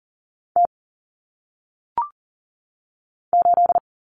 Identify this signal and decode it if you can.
Morse code, heard as a keyed tone, about 40 wpm, 700 Hz. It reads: T 8
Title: Common Cut Numbers - 1-Digit 40wpm